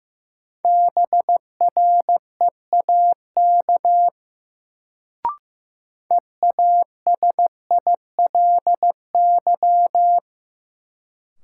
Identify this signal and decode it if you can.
Morse code, heard as a keyed tone, about 15 wpm, 700 Hz. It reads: BREAK EASILY
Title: Sentences from Top 900 Words - New Word Drill - 15wpm